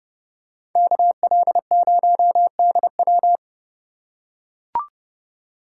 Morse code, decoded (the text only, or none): KL0DW